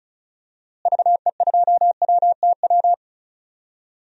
Morse code, heard as a keyed tone, about 36 wpm, 700 Hz. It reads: VE2WTW
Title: DXCC-100 Call Signs - 6 Characters - Repeated 2x - 35wpm